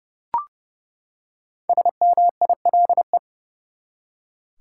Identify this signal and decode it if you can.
Morse code, heard as a keyed tone, about 32 wpm, 700 Hz. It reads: SMILE
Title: Five Letter Words - 30wpm